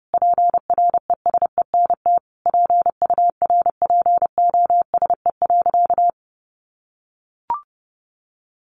Morse code, heard as a keyed tone, about 30 wpm, 700 Hz. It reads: PRESENT PURPOSE.
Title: Sentences from Top 1400 Words 30wpm